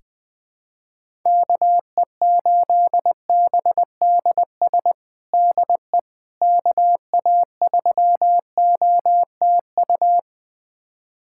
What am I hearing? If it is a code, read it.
Morse code, heard as a keyed tone, about 21 wpm, 700 Hz. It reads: KE8BDS DE KA3OTU
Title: USA Callsign DE Callsign 20wpm